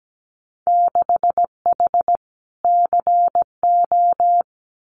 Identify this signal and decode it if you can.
Morse code, heard as a keyed tone, about 17 wpm, 700 Hz. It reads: 6H CO
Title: Winter Field Day Exchanges Encore 17wpm